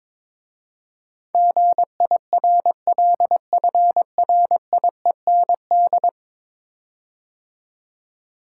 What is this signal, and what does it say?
Morse code, heard as a keyed tone, about 22 wpm, 700 Hz. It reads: GIRLFRIEND
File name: practice / 22wpm / 10 Letter Words - First 3 Letters Given - 22wpm